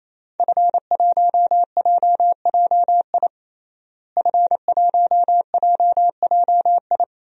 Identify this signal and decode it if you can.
Morse code, heard as a keyed tone, about 28 wpm, 700 Hz. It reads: F1JJI F1JJI
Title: DXCC-100 Call Signs - 5 Characters - Repeated 3x - 28wpm